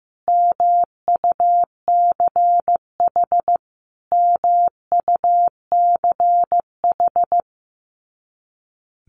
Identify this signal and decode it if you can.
Morse code, heard as a keyed tone, about 15 wpm, 700 Hz. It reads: MUCH MUCH
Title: Top 300 Words Repeated 15wpm